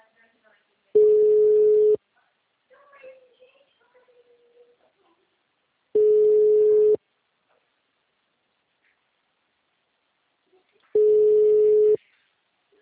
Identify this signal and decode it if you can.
Ringback tone (Europe, 425 Hz), with speech.